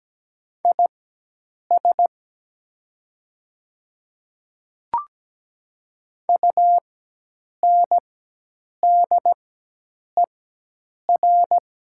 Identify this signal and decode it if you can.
Morse code, heard as a keyed tone, about 17 wpm, spaced as for 8 wpm, 700 Hz. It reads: IS UNDER
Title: 4x Character Spacing - Top 200 Words - 17wpm